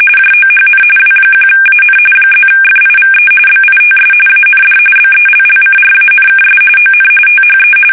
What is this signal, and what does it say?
RTTY, about 45.45 baud. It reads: JOHN W1AN +FIVE BITS AND BELLCRANKS ON THE
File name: fsk850 / USS-Massachusetts